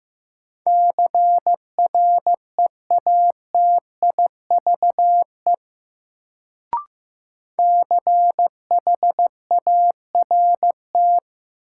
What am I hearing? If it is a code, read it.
Morse code, heard as a keyed tone, about 16 wpm, 700 Hz. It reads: CREATIVE CHART
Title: Sentences from Top 1300 Words - New Word Drill Alt - 15wpm